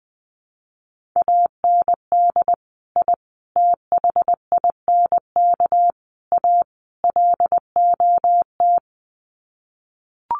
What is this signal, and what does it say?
Morse code, heard as a keyed tone, about 20 wpm, 700 Hz. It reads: AND I THINK A LOT
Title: Sets of 5 Words Encore2 20wpm